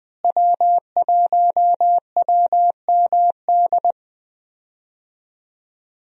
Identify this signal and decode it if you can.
Morse code, heard as a keyed tone, about 20 wpm, 700 Hz. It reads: W1WMD
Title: USA Call Signs - 5 Characters - Repeated 3x - 20wpm